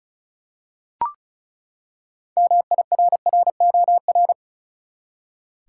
Morse code, heard as a keyed tone, about 36 wpm, 700 Hz. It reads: MIRROR
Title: Six Letter Words - 35wpm